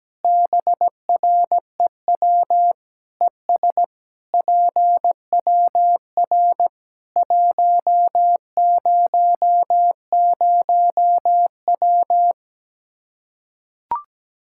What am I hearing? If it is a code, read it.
Morse code, heard as a keyed tone, about 17 wpm, 700 Hz. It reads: BREW ES PWR 100W
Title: Practice QSO - Exchange 2 - Info - 17wpm